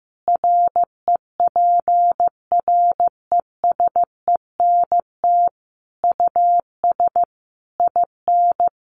Morse code, heard as a keyed tone, about 15 wpm, 700 Hz. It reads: REPRESENT US IN
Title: Sentences from Top 1200 Words 15wpm